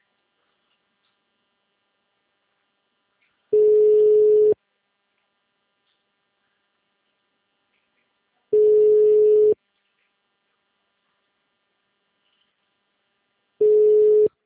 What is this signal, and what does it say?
Ringback tone (Europe, 425 Hz), with speech.
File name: out- -2002- - - .36